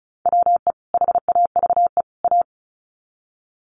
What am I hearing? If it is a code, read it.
Morse code, heard as a keyed tone, about 34 wpm, 700 Hz. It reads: WE HAVE A
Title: Sets of 3 Words 35wpm